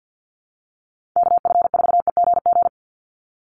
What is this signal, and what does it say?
Morse code, heard as a keyed tone, about 50 wpm, 700 Hz. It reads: KF4EGG
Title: USA Call Signs 50wpm